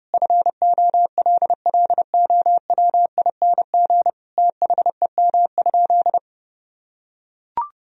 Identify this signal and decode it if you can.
Morse code, heard as a keyed tone, about 30 wpm, 700 Hz. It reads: FOLLOWING THEM?
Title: Sentences from Top 1000 Words - Review Part 4 - 30wpm